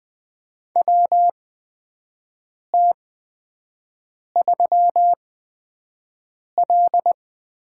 Morse code, heard as a keyed tone, about 20 wpm, spaced as for 5 wpm, 700 Hz. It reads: WT3L
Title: USA Call Signs - 4 Characters - 8x Character Spacing - 20wpm